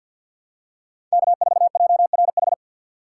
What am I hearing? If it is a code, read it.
Morse code, heard as a keyed tone, about 46 wpm, 700 Hz. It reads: K4JRH
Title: USA Call Signs - 5 Characters - 3x Character Spacing - 50wpm